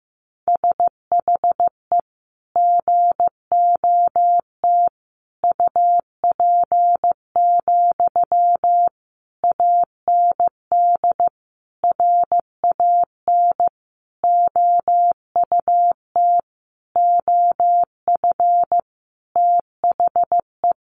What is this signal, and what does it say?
Morse code, heard as a keyed tone, about 15 wpm, 700 Hz. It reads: SHE GOT UP, AND RAN OUT OF THE
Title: Sentences from Top 500 Words 15wpm